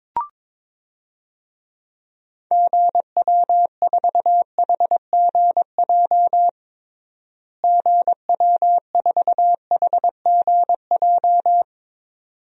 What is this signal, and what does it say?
Morse code, heard as a keyed tone, about 22 wpm, 700 Hz. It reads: GW4HGJ GW4HGJ
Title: DXCC-100 Call Signs - 6 Characters - Repeated 2x - 22wpm